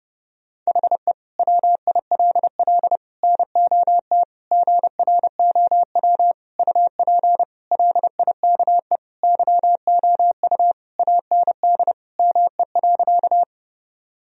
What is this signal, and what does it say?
Morse code, heard as a keyed tone, about 30 wpm, 700 Hz. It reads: HE WILL NOT GROW UP LIKE YOU AND ME.
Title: Sentences from Top 1000 Words - Review Part 1 - 30wpm